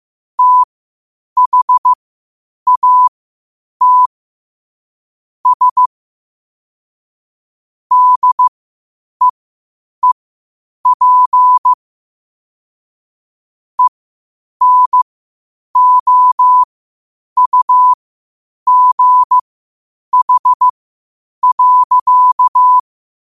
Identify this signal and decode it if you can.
Morse code, heard as a keyed tone, about 15 wpm, spaced as for 8 wpm, 1000 Hz. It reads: THAT S DEEP ENOUGH.
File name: qotdfiles / 8wpm